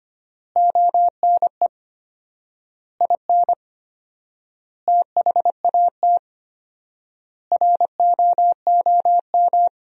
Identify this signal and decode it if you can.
Morse code, heard as a keyed tone, about 25 wpm, 700 Hz. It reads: ONE IN THAT ROOM
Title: 4x Word-Spacing - Sentences from Top 500 Words - 25wpm